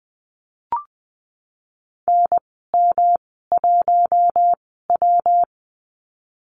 Morse code, heard as a keyed tone, about 20 wpm, spaced as for 15 wpm, 700 Hz. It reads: NM1W
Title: 167 Lesson 27g - Number 7 - Call Signs 20wpm